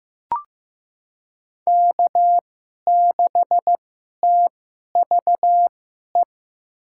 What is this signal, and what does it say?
Morse code, heard as a keyed tone, about 15 wpm, spaced as for 11 wpm, 700 Hz. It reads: K6TVE
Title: USA Call Signs - 2x Character Spacing - 15wpm